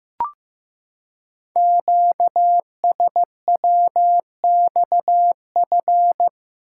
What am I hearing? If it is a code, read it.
Morse code, heard as a keyed tone, about 15 wpm, 700 Hz. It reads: QSWXF
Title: Code Groups - 5 Letters - 15wpm